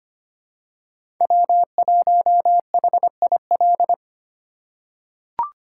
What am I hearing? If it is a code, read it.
Morse code, heard as a keyed tone, about 25 wpm, 700 Hz. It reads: W1HIL
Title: USA Call Signs - 3x Character Spacing - 25wpm